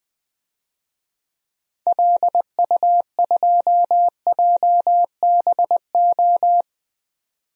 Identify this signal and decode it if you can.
Morse code, heard as a keyed tone, about 20 wpm, 700 Hz. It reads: LU2JBO
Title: DXCC-100 Call Signs - 6 Characters - 20wpm